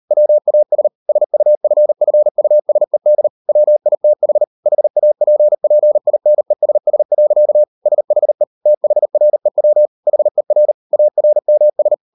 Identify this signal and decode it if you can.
Morse code, heard as a keyed tone, about 40 wpm, 600 Hz. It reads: WAS SUFFUSED WITH HAPPINESS. SHE THREW HER ARMS